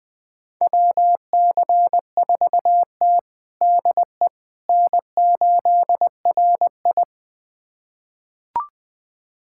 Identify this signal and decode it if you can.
Morse code, heard as a keyed tone, about 21 wpm, 700 Hz. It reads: WC4T DE N8RI
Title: USA Callsign DE Callsign - 4 Characters - 20wpm